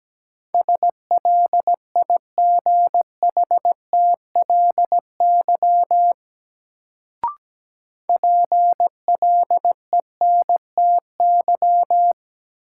Morse code, heard as a keyed tone, about 17 wpm, 700 Hz. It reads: SLIGHTLY PLENTY
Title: Sentences from Top 1300 Words - New Word Drill - 17wpm